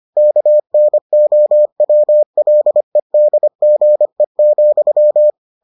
Morse code, heard as a keyed tone, about 25 wpm, 600 Hz. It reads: KNOWLEDGE,